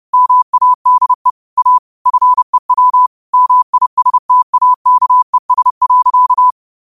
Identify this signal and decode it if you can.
Morse code, heard as a keyed tone, about 30 wpm, 1000 Hz. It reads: MADE A FEW MISTAKES.
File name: qotdfiles / 30wpm